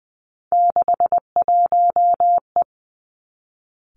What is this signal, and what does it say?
Morse code, heard as a keyed tone, about 20 wpm, 700 Hz. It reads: 61E
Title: Three Challenging Letter-Number Combinations 20wpm